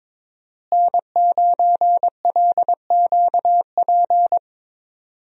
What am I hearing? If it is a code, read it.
Morse code, heard as a keyed tone, about 22 wpm, 700 Hz. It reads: N9LQP
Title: USA Call Signs - 5 Characters - Repeated 3x - 22wpm